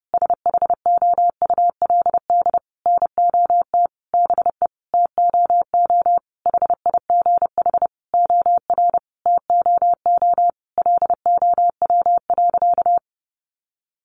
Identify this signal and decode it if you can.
Morse code, heard as a keyed tone, about 28 wpm, 700 Hz. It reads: SHOULD NOT BE TOO HIGH OR TOO LOW.
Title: Sentences from Top 1200 Words 30wpm